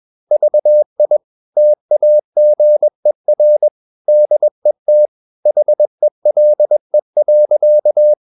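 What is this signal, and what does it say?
Morse code, heard as a keyed tone, about 21 wpm, 600 Hz. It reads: VI TAGER DET HELE.
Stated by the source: Morse practice at 21 wpm, slow-speed holiday exercise.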